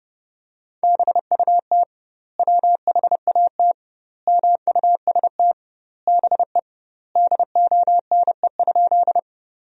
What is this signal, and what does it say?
Morse code, heard as a keyed tone, about 32 wpm, 700 Hz. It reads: BUT WHAT MUST BE DONE?
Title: 2x Word-Spacing - Sentences from Top 500 Words - 30wpm